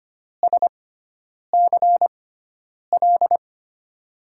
Morse code, heard as a keyed tone, about 25 wpm, spaced as for 9 wpm, 700 Hz. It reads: SCL
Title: USA Call Signs - 6 Characters - 6x Character Spacing - 25wpm